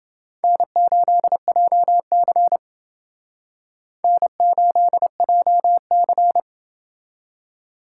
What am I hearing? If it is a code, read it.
Morse code, heard as a keyed tone, about 28 wpm, 700 Hz. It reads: N8JC N8JC